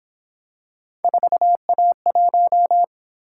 Morse code, heard as a keyed tone, about 26 wpm, 700 Hz. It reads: 4A1
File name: Speed-Racing - Three Letter-Number Combinations - 22wpm